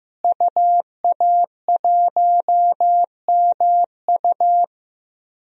Morse code, heard as a keyed tone, about 15 wpm, 700 Hz. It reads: UA1MU
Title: DXCC-100 Call Signs - 5 Characters - Repeated 3x - 15wpm